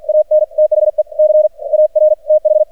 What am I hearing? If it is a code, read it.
Morse code, heard as a keyed tone, about 48 wpm, spaced as for 39 wpm, 610 Hz. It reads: AN THE CAR TH
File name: archived / CRSnet